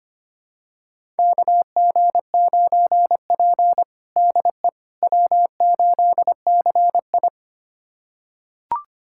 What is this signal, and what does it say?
Morse code, heard as a keyed tone, about 25 wpm, 700 Hz. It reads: KG9P DE W8CI
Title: USA Callsign DE Callsign - 4 Characters - 25wpm